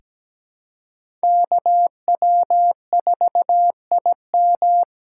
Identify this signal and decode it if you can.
Morse code, heard as a keyed tone, about 17 wpm, 700 Hz. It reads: KW4IM
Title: USA Call Signs - 5 Characters - 3x Character Spacing - 17wpm